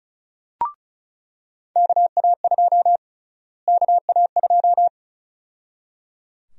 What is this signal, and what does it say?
Morse code, heard as a keyed tone, about 36 wpm, 700 Hz. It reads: KA2 KA2
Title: USA Call Sign Prefixes - Repeated 2x - 35wpm